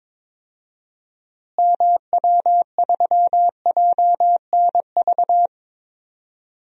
Morse code, heard as a keyed tone, about 22 wpm, 700 Hz. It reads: MW3JNV
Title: DXCC-100 Call Signs - 6 Characters - Repeated 3x - 22wpm